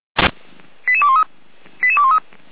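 Five-tone selective call (ZVEI), heard as quick runs of five tones, tones of about 70 ms. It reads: 88213 88213
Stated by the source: ZVEI five-tone selective call received by SDR.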